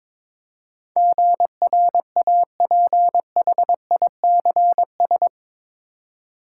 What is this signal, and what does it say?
Morse code, heard as a keyed tone, about 22 wpm, 700 Hz. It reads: GRAPHICS
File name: practice / 22wpm / Eight Letter Words - 22wpm